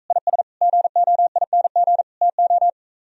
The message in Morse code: IS GOING TO

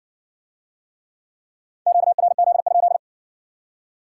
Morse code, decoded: QRZ?

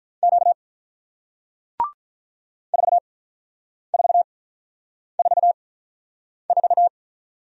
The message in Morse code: Q 4444

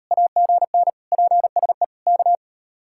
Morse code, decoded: AGN PSE K

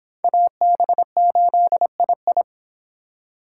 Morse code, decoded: AB8II

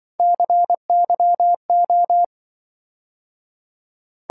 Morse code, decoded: CYO

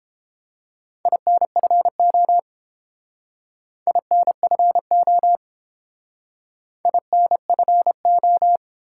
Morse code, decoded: INFO INFO INFO

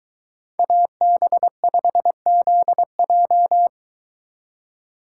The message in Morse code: AB5ZJ

